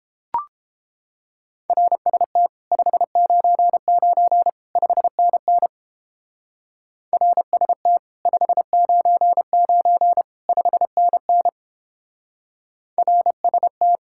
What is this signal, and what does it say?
Morse code, heard as a keyed tone, about 30 wpm, 700 Hz. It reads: RST 599 5NN RST 599 5NN RST